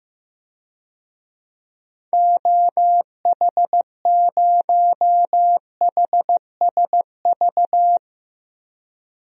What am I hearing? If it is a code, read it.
Morse code, heard as a keyed tone, about 15 wpm, 700 Hz. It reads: OH0HSV
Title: DXCC-100 Call Signs - 6 Characters - Repeated 2x - 15wpm